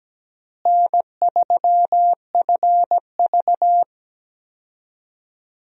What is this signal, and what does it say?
Morse code, heard as a keyed tone, about 17 wpm, 700 Hz. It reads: N3FV
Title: USA Call Signs - 4 Characters - Repeated 3x - 17wpm